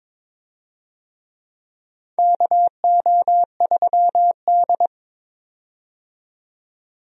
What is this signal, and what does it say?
Morse code, heard as a keyed tone, about 22 wpm, 700 Hz. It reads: KO3D
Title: USA Call Signs - 4 Characters - 22wpm